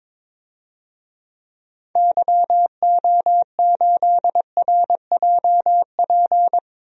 Morse code, heard as a keyed tone, about 21 wpm, 700 Hz. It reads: YO8RJP
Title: DXCC-100 Call Signs - 6 Characters - 22wpm